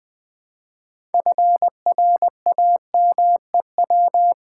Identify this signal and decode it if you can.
Morse code, heard as a keyed tone, about 20 wpm, 700 Hz. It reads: FRAMEW
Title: Build A Word - 3 Letters At A Time - 9-Letter Words - 20wpm